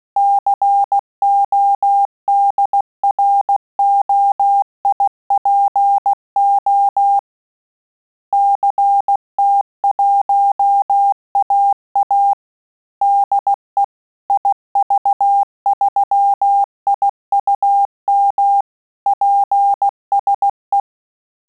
Morse code, heard as a keyed tone, about 16 wpm, 800 Hz. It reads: CODROIPO CT1AA DE IV3IUM PSE
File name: Telegraphy-group / QSO CW 15 PPM